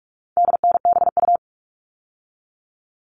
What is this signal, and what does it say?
Morse code, heard as a keyed tone, about 44 wpm, 700 Hz. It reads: DNBU
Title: All Cut Numbers - 4-Digit 45wpm